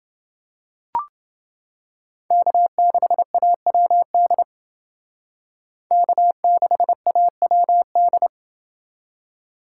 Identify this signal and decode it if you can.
Morse code, heard as a keyed tone, about 29 wpm, 700 Hz. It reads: K6AWD K6AWD